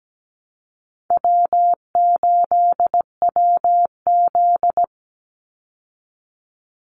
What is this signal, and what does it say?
Morse code, heard as a keyed tone, about 17 wpm, 700 Hz. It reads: W8WZ
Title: Easy Call Signs Repeated 3x - 17wpm